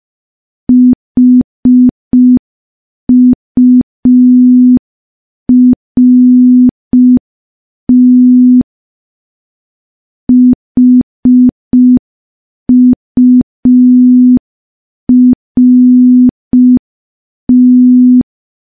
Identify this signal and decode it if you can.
Morse code, heard as a keyed tone, about 5 wpm, 250 Hz. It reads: HURT HURT
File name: morse / morse-training-05WPM